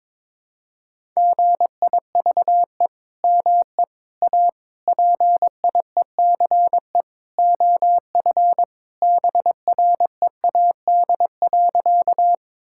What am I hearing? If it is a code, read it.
Morse code, heard as a keyed tone, about 22 wpm, 700 Hz. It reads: GIVE ME A PIECE OF BREAD.
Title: Sentences from Top 1600 Words 22wpm